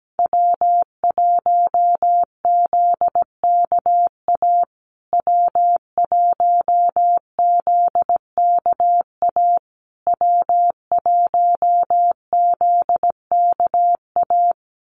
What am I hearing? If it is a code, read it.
Morse code, heard as a keyed tone, about 17 wpm, 700 Hz. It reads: W1ZKA W1ZKA W1ZKA
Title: USA Call Signs Repeated 3x 17wpm